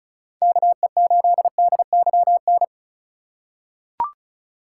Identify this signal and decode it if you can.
Morse code, heard as a keyed tone, about 34 wpm, 700 Hz. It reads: KE8DYN
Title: USA Call Signs - 6 Characters - 4x Character Spacing - 35wpm